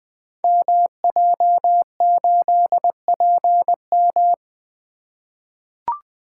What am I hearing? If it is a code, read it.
Morse code, heard as a keyed tone, about 20 wpm, 700 Hz. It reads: MJ8PM
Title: Code Groups - 5 Letter:Numbers - 20wpm